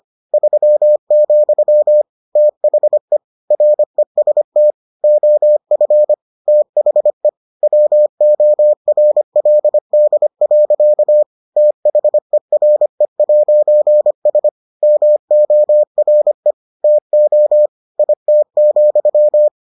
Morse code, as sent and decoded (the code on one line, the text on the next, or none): ...-- --..-- / - .... / .-. ... - / --- ..-. / - .... / .-- --- .-. .-.. -.. .-.-.- / - .... .-. .----. ... / -- --- .-. / - --- / .. - --..--
3, THE REST OF THE WORLD. THERE'S MORE TO IT,